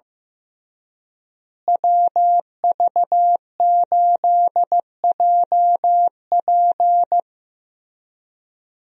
Morse code, heard as a keyed tone, about 15 wpm, 700 Hz. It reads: WV8JP
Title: USA Call Signs - 5 Characters - Repeated 3x - 15wpm